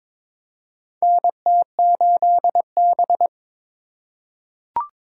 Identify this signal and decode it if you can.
Morse code, heard as a keyed tone, about 22 wpm, 700 Hz. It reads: NT8B
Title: USA Call Signs - 4 Characters - 6x Character Spacing - 22wpm